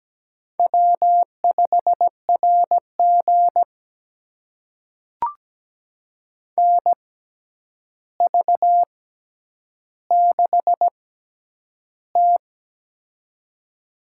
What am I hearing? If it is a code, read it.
Morse code, heard as a keyed tone, about 17 wpm, spaced as for 9 wpm, 700 Hz. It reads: W5RG NV6T
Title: USA Call Signs - 4 Characters - 6x Character Spacing - 17wpm